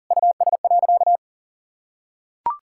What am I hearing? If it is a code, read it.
Morse code, heard as a keyed tone, about 40 wpm, 700 Hz. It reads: US.
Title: Sentences from Top 1000 Words - Review Part 5 - 40wpm